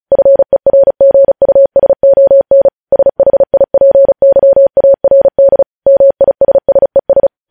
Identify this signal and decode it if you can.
Morse code, heard as a keyed tone, about 36 wpm, 550 Hz. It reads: FERGUSON SHIPYARD MISSES
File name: podcast / News headlines-1x-35x35